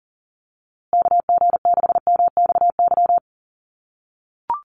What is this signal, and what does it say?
Morse code, heard as a keyed tone, about 38 wpm, 700 Hz. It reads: KG6MXY
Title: USA Call Signs Repeated 3x 40wpm